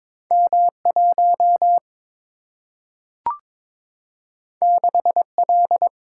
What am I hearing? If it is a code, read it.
Morse code, heard as a keyed tone, about 22 wpm, 700 Hz. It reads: M 1 6 L
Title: Single-Letter Single-Number 22wpm